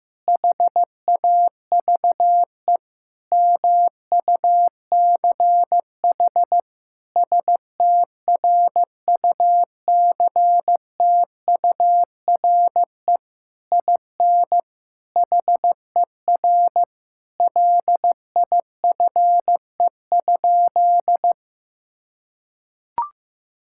Morse code, heard as a keyed tone, about 16 wpm, 700 Hz. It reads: HAVE MUCH STRUCTURE IN HER LIFE?
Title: Sentences from Top 1000 Words - Review Part 6 - 15wpm